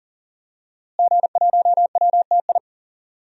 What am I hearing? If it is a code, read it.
Morse code, heard as a keyed tone, about 40 wpm, 700 Hz. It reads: G1WTI
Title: DXCC-100 Call Signs - 5 Characters - Repeated 2x - 40wpm